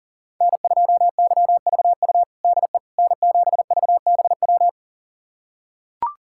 Morse code, heard as a keyed tone, about 40 wpm, 700 Hz. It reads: N2YVU DE N7VBW
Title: USA Callsign DE Callsign 40wpm